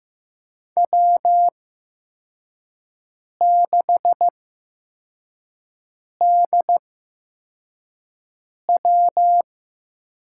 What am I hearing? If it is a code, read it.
Morse code, heard as a keyed tone, about 15 wpm, spaced as for 4 wpm, 700 Hz. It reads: W6DW
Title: USA Call Signs - 8x Character Spacing - 15wpm